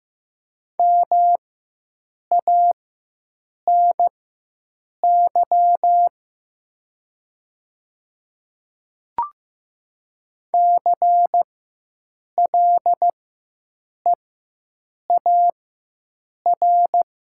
Morse code, heard as a keyed tone, about 15 wpm, spaced as for 7 wpm, 700 Hz. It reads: MANY CLEAR
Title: 4x Character Spacing - Top 500 Words - 15wpm